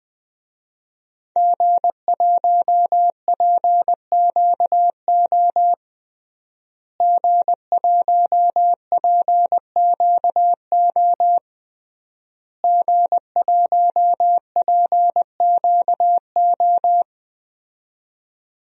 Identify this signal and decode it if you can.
Morse code, heard as a keyed tone, about 20 wpm, 700 Hz. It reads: G1PQO G1PQO G1PQO
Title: DXCC-100 Call Signs - 5 Characters - Repeated 3x - 20wpm